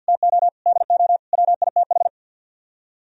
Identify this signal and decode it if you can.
Morse code, heard as a keyed tone, about 62 wpm, 700 Hz. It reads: TO DO WITH
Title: Speed-Racing - Sets of 4 Words 50wpm